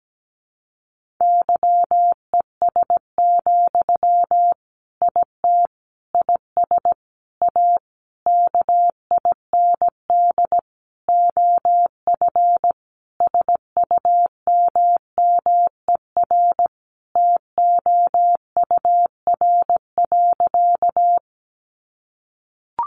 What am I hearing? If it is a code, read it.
Morse code, heard as a keyed tone, about 17 wpm, 700 Hz. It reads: YES, IT IS A KIND OF SUMMER TOUR.
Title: Sentences from Top 1400 Words 17wpm